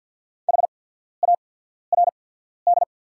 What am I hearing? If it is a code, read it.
Morse code, heard as a keyed tone, about 48 wpm, spaced as for 14 wpm, 690 Hz. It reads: HARD
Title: 8x Character Spacing - Top 300 Words - 50wpm